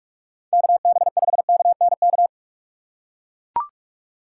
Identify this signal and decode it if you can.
Morse code, heard as a keyed tone, about 44 wpm, 700 Hz. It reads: KB5KNK
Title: USA Call Signs - 6 Characters - 6x Character Spacing - 45wpm